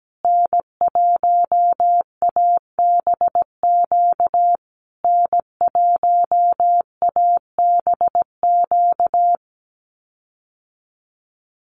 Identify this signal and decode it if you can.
Morse code, heard as a keyed tone, about 17 wpm, 700 Hz. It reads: N1ABQ N1ABQ
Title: USA Call Signs Repeated 3x 17wpm